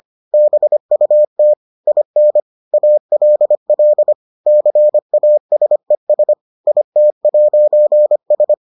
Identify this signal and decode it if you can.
Morse code, heard as a keyed tone, about 25 wpm, 600 Hz. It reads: BUT IN ALL CASES IT'S